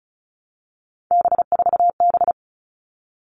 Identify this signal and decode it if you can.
Morse code, heard as a keyed tone, about 34 wpm, 700 Hz. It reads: B4B
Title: All Cut Numbers - 3-Digit 35wpm